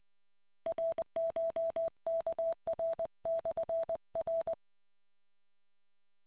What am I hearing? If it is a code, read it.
Morse code, heard as a keyed tone, about 20 wpm, 660 Hz. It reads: RCHKR/R